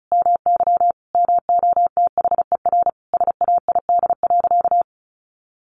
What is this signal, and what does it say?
Morse code, heard as a keyed tone, about 34 wpm, 700 Hz. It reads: MY MOTHER SAID.
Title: Sentences from Top 700 Words 35wpm